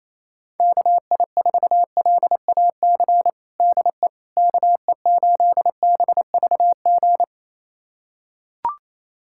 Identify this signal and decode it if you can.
Morse code, heard as a keyed tone, about 28 wpm, 700 Hz. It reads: KI4LAC DE KE8BVG
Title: USA Callsign DE Callsign - 6 Characters - 28wpm